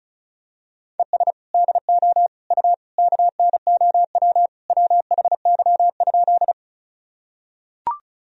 Morse code, heard as a keyed tone, about 34 wpm, 700 Hz. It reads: ES DO U KNOW WHY?
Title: Sentences from Top 200 Words and Abbreviations 35wpm